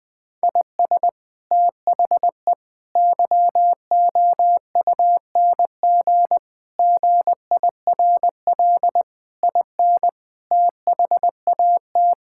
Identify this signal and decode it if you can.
Morse code, heard as a keyed tone, about 20 wpm, 700 Hz. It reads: IS THE YOUNG GIRL IN THAT
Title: Sentences from Top 1000 Words - Review Part 5 - 20wpm